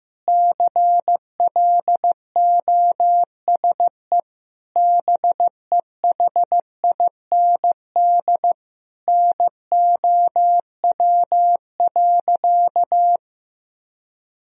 Morse code, heard as a keyed tone, about 16 wpm, 700 Hz. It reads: CLOSE BEHIND NOW.
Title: Sentences from Top 1000 Words - Review Part 1 - 15wpm